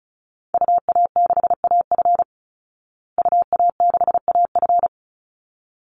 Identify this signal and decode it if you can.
Morse code, heard as a keyed tone, about 34 wpm, 700 Hz. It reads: UA6AF UA6AF
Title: Call Signs Repeated 3x - 35wpm